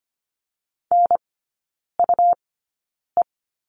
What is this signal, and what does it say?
Morse code, heard as a keyed tone, about 25 wpm, spaced as for 9 wpm, 700 Hz. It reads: NUE
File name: Lesson 17e - Letter u - Three Characters 25wpm